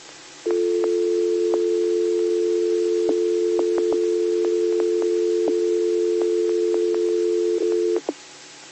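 Dial tone (North America), 350 + 440 Hz.
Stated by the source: Received on 2m at 146.115 MHz.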